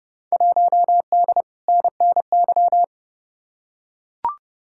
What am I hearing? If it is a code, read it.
Morse code, heard as a keyed tone, about 32 wpm, 700 Hz. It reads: 1D NNY
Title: Field Day Exchanges Encore - 30wpm